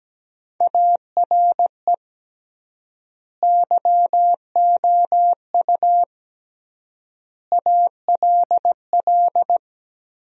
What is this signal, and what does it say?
Morse code, heard as a keyed tone, about 17 wpm, 700 Hz. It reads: ARE YOU ALL
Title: 3x Word-Spacing - Sentences from Top 400 Words - 17wpm